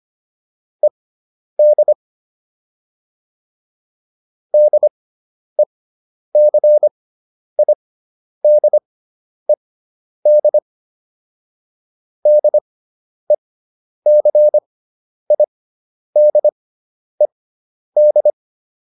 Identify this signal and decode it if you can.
Morse code, heard as a keyed tone, about 25 wpm, spaced as for 10 wpm, 600 Hz. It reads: ED DECIDED DECIDED